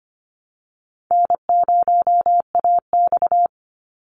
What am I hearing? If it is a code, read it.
Morse code, heard as a keyed tone, about 25 wpm, 700 Hz. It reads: N0AX
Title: Easy Call Signs Repeated 3x - 25wpm